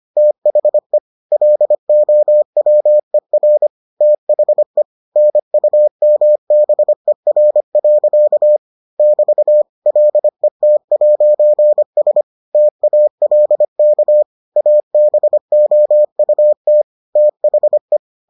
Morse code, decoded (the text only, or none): THE LOWER THE NUMBER. <BT> LET'S TALK ABOUT THE